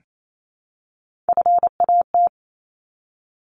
Morse code, heard as a keyed tone, about 28 wpm, 700 Hz. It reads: FAT
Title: Top 501-750 Words 28wpm